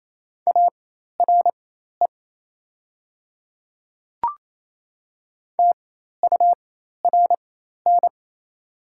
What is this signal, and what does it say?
Morse code, heard as a keyed tone, about 28 wpm, spaced as for 13 wpm, 700 Hz. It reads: ARE TURN
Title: 4x Character Spacing - Top 500 Words - 28wpm